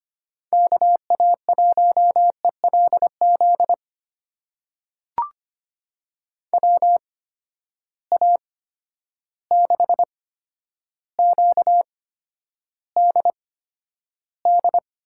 Morse code, heard as a keyed tone, about 25 wpm, spaced as for 11 wpm, 700 Hz. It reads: KA1ELZ WA6QDD
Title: USA Call Signs - 6 Characters - 8x Character Spacing - 25wpm